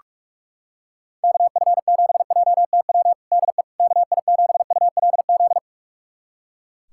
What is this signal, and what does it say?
Morse code, heard as a keyed tone, about 41 wpm, 700 Hz. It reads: KF7JTW DE KI7ULZ
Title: USA Callsign DE Callsign 45wpm